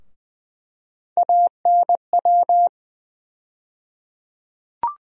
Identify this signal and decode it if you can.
Morse code, heard as a keyed tone, about 20 wpm, 700 Hz. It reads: ANW
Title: USA Call Sign Postfixes - 3 Letters 20wpm